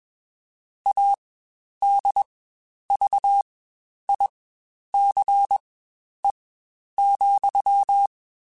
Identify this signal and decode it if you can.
Morse code, heard as a keyed tone, about 21 wpm, spaced as for 10 wpm, 800 Hz. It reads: ADVICE,
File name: Quotes Spaced